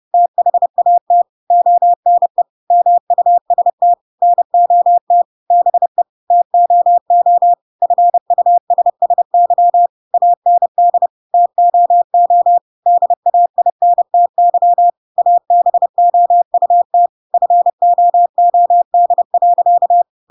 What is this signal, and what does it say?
Morse code, heard as a keyed tone, about 30 wpm, 700 Hz. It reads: THAT ONE MUST NOT BE TOO FUSSY AND TOO DAINTY ABOUT FOOD.
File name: Pinocchio / Pinocchio ch7 30 30 ik1hge